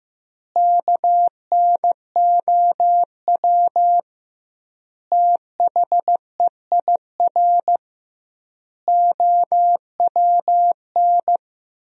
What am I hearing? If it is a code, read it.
Morse code, heard as a keyed tone, about 15 wpm, 700 Hz. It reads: KNOW THEIR OWN
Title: 2x Word-Spacing - Sentences from Top 300 Words - 15wpm